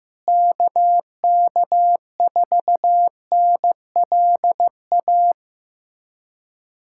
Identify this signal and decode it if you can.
Morse code, heard as a keyed tone, about 15 wpm, 700 Hz. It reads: KK4NLA
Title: USA Callsign DE Callsign - 6 Characters - 15wpm